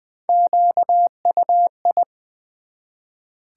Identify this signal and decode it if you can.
Morse code, heard as a keyed tone, about 20 wpm, 700 Hz. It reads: QUI
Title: Build A Word - 1 Letter At A Time - 3 to 5-Letter Words - 20wpm